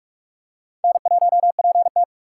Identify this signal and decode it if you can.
Morse code, heard as a keyed tone, about 42 wpm, 700 Hz. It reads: N1PT
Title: USA Call Signs - 4 Characters - 2x Character Spacing - 45wpm